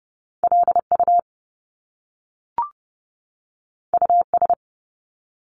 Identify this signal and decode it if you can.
Morse code, heard as a keyed tone, about 30 wpm, 700 Hz. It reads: LU US
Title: Top Two Letter Combinations in Words 30wpm